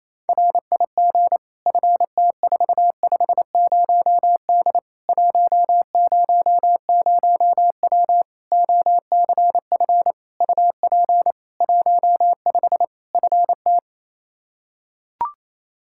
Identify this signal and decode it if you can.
Morse code, heard as a keyed tone, about 28 wpm, 700 Hz. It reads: RIG FT450D 100W OCF UP 15 FT
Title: Practice QSO - Exchange 2 - Info - 28wpm